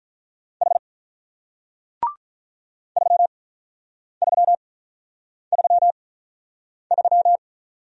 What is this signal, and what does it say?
Morse code, heard as a keyed tone, about 48 wpm, spaced as for 9 wpm, 700 Hz. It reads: H 3333